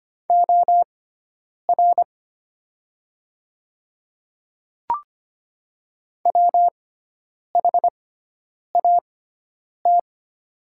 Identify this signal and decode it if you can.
Morse code, heard as a keyed tone, about 25 wpm, spaced as for 9 wpm, 700 Hz. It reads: OR WHAT
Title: 6x Character Spacing - Top 300 Words - 25wpm